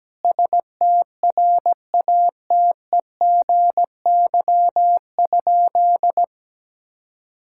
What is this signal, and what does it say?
Morse code, heard as a keyed tone, about 17 wpm, 700 Hz. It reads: STRATEGY?
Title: Sentences from Top 1000 Words - Review Part 5 - 17wpm